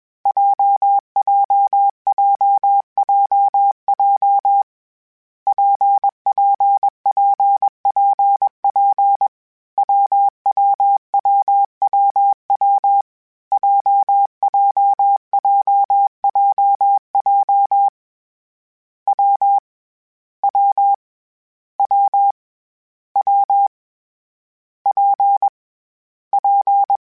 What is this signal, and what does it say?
Morse code, heard as a keyed tone, about 21 wpm, 800 Hz. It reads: JJJJJ PPPPP WWWWW JJJJJ W W W W P P
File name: Variable Speed / Variable Speed-mp3s / WPJ WPJ1 sp @20